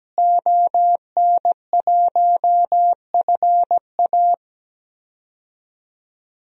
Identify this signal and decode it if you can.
Morse code, heard as a keyed tone, about 17 wpm, 700 Hz. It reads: ON1FA
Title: DXCC-100 Call Signs - 5 Characters - Repeated 2x - 17wpm